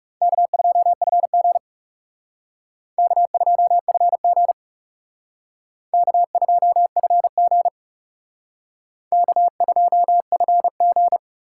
Morse code, heard as a keyed tone, about 34 wpm, 700 Hz. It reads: K2FG K2FG K2FG K2FG
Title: Speed Racing - USA 1x2 Call Signs 30wpm